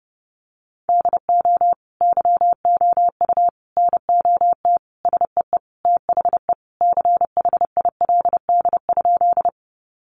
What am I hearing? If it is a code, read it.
Morse code, heard as a keyed tone, about 30 wpm, 700 Hz. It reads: DO YOU NOT SEE THE CHILD?
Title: Sentences from Top 800 Words 30wpm